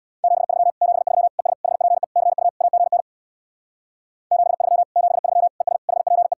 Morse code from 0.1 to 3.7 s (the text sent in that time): 73 73 ES HPE CU AGN